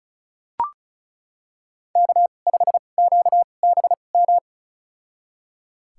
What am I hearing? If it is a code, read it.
Morse code, heard as a keyed tone, about 36 wpm, spaced as for 25 wpm, 700 Hz. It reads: K5QBM
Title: USA Call Signs - 5 Characters - 2x Character Spacing - 35wpm